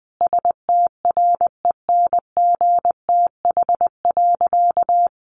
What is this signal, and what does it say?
Morse code, heard as a keyed tone, about 20 wpm, 700 Hz. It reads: STRENGTH.
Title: Sentences from Top 1300 Words 20wpm